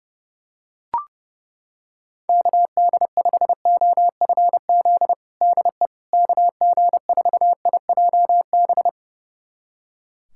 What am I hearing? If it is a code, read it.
Morse code, heard as a keyed tone, about 31 wpm, 700 Hz. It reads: KD5OFZ DE KG4IJB
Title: USA Callsign DE Callsign - 6 Characters - 30wpm